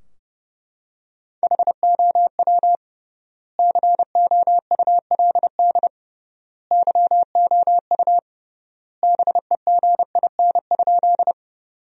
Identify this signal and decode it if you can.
Morse code, heard as a keyed tone, about 30 wpm, 700 Hz. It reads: HOW COULD YOU BEGIN?
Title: 3x Word-Spacing - Sentences from Top 300 Words - 30wpm